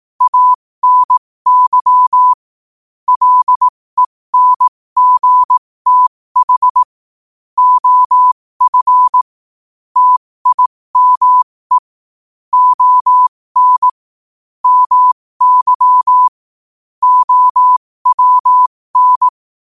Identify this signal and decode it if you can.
Morse code, heard as a keyed tone, about 18 wpm, spaced as for 15 wpm, 1000 Hz. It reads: ANY LENGTH OF TIME ON MY OWN
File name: qotdfiles / 15wpm